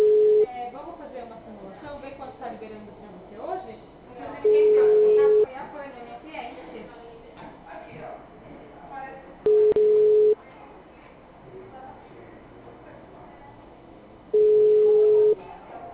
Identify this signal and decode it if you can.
Ringback tone (Europe, 425 Hz), with speech.
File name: out- -2014- - - .38603